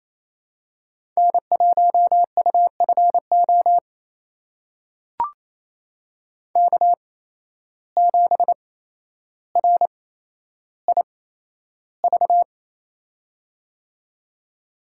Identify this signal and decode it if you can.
Morse code, heard as a keyed tone, about 28 wpm, spaced as for 12 wpm, 700 Hz. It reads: N1UFO K7RIV
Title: USA Call Signs - 8x Character Spacing - 28wpm